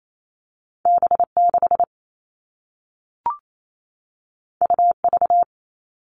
Morse code, heard as a keyed tone, about 28 wpm, 700 Hz. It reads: B6 UV
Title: All Cut Numbers - 2-Digit 28wpm